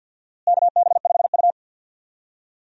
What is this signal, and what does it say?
Morse code, heard as a keyed tone, about 52 wpm, 700 Hz. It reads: KB5U
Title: USA Call Signs - 4 Characters - Repeated 3x - 50wpm